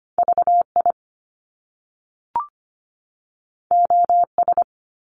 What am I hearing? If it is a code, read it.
Morse code, heard as a keyed tone, about 25 wpm, 700 Hz. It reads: VI OS
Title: Top Two Letter Combinations in Words 25wpm